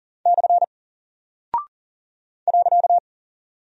Morse code, heard as a keyed tone, about 40 wpm, 700 Hz. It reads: / .
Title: Single Character (Letter:Number:Punctuation:Prosign) Alt 40wpm